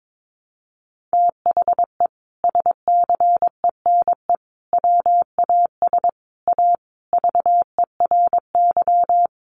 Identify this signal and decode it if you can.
Morse code, heard as a keyed tone, about 22 wpm, 700 Hz. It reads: THE SCENE WAS A VERY
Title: Sentences from Top 1100 Words 22wpm